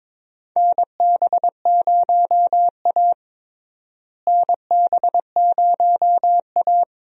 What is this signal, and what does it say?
Morse code, heard as a keyed tone, about 22 wpm, 700 Hz. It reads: NB0A NB0A
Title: USA Call Signs - 4 Characters - Repeated 3x - 22wpm